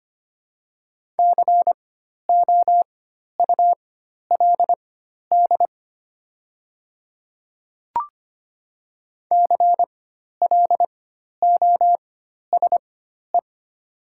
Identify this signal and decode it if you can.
Morse code, heard as a keyed tone, about 25 wpm, spaced as for 12 wpm, 700 Hz. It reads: COULD CLOSE
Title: 4x Character Spacing - Top 300 Words - 25wpm